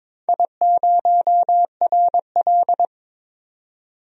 Morse code, heard as a keyed tone, about 21 wpm, 700 Hz. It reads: I0RL
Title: DXCC-100 Call Signs - 4 Characters - 22wpm